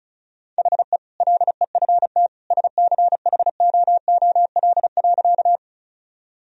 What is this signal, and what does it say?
Morse code, heard as a keyed tone, about 34 wpm, 700 Hz. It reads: HE LEFT SCHOOL.
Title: Sentences from Top 1000 Words - Review Part 1 - 35wpm